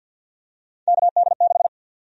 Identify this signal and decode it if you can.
Morse code, heard as a keyed tone, about 48 wpm, 700 Hz. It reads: KD6
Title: USA Call Sign Prefixes - 50wpm